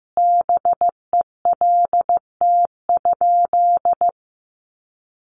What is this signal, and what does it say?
Morse code, heard as a keyed tone, about 15 wpm, 700 Hz. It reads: BELT?
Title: Sentences from Top 1400 Words 15wpm